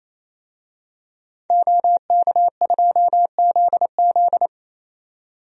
Morse code, heard as a keyed tone, about 28 wpm, 700 Hz. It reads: OK2ZZ
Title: DXCC-100 Call Signs - 5 Characters - 28wpm